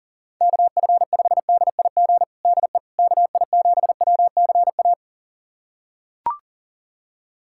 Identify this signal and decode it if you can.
Morse code, heard as a keyed tone, about 37 wpm, 700 Hz. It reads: KF5DIG DE KI7WCA
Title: USA Callsign DE Callsign 40wpm